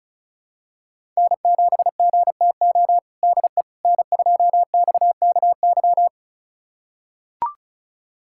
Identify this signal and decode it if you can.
Morse code, heard as a keyed tone, about 34 wpm, 700 Hz. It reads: N7GTO DE N2XKY
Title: USA Callsign DE Callsign - 5 Characters - 35wpm